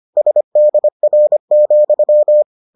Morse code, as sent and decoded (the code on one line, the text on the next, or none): ... -.. .-. --..--
SDR,